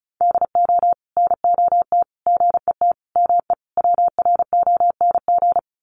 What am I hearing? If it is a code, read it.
Morse code, heard as a keyed tone, about 34 wpm, 700 Hz. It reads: DO NOT GET ME WRONG
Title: Sets of 5 Words Encore 35wpm